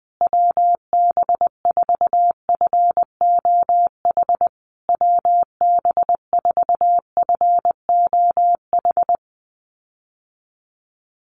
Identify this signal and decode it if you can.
Morse code, heard as a keyed tone, about 20 wpm, 700 Hz. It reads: WB4FOH WB4FOH
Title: USA Call Signs Repeated 3x 20wpm